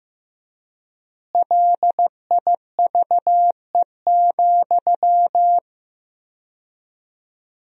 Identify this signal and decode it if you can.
Morse code, heard as a keyed tone, about 15 wpm, 700 Hz. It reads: LIVE,
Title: 4x Word-Spacing - Sentences from Top 200 Words - 15wpm